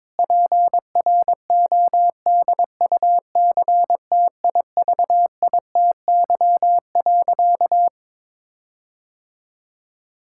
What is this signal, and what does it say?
Morse code, heard as a keyed tone, about 22 wpm, 700 Hz. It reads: PRODUCTIVITY.
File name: Difficult Sentences with Hints - 22wpm